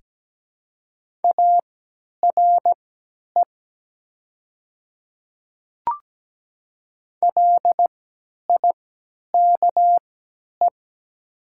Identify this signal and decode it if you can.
Morse code, heard as a keyed tone, about 17 wpm, spaced as for 10 wpm, 700 Hz. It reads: ARE LIKE
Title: 3x Character Spacing - Top 100 Words - 17wpm